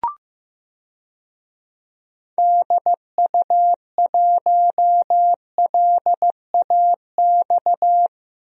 Morse code, heard as a keyed tone, about 15 wpm, 700 Hz. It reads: DU1LAX
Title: DXCC-100 Call Signs - 6 Characters - 15wpm